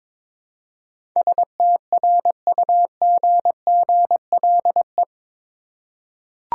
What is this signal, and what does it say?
Morse code, heard as a keyed tone, about 22 wpm, 700 Hz. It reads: STRUGGLE
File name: practice / 22wpm / Eight Letter Words - 22wpm